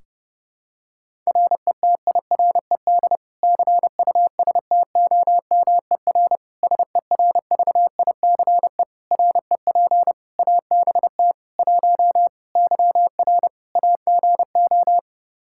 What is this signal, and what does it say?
Morse code, heard as a keyed tone, about 30 wpm, 700 Hz. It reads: RETIRED CUSTOMER SERVICE REP ABT 1 YR AGO
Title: Practice QSO - Exchange 3 - More Info - 30wpm